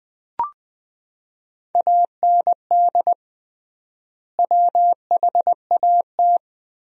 Morse code, heard as a keyed tone, about 20 wpm, 700 Hz. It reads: AND WHAT
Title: 3x Word-Spacing - Sentences from Top 100 Words - 20wpm